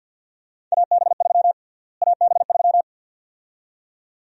Morse code, heard as a keyed tone, about 51 wpm, 700 Hz. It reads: AB3 AB3
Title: USA Call Sign Prefixes - Repeated 2x - 50wpm